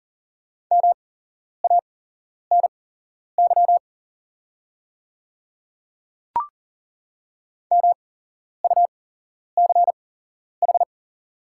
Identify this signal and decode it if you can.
Morse code, heard as a keyed tone, about 40 wpm, spaced as for 11 wpm, 700 Hz. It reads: MANY MUCH
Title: 8x Character Spacing - Top 300 Words - 40wpm